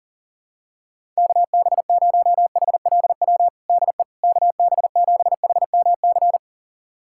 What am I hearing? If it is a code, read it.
Morse code, heard as a keyed tone, about 40 wpm, 700 Hz. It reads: KB0HLW DE KB7HMC
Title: USA Callsign DE Callsign - 6 Characters - 40wpm